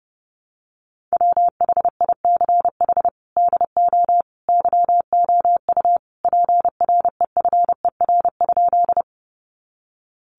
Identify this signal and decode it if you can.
Morse code, heard as a keyed tone, about 30 wpm, 700 Hz. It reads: WHICH DO YOU PREFER?
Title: Sentences from Top 1100 Words 30wpm